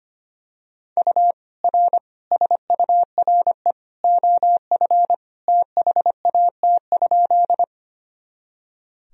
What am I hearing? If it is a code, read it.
Morse code, heard as a keyed tone, about 25 wpm, 700 Hz. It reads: U R SURE OF THAT?
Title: Sentences from Top 300 Words and Abbreviations 25wpm